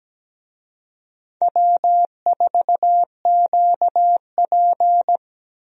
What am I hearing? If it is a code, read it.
Morse code, heard as a keyed tone, about 17 wpm, 700 Hz. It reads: W4QP
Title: USA Call Signs - 4 Characters - Repeated 3x - 17wpm